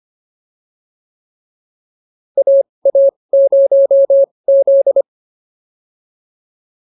Morse code, heard as a keyed tone, about 25 wpm, spaced as for 20 wpm, 550 Hz. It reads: AA0Z